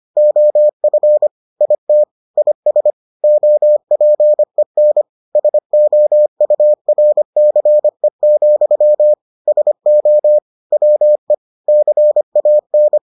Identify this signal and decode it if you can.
Morse code, heard as a keyed tone, about 25 wpm, 600 Hz. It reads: OF IT IS OPEN SOURCE, SO WE CAN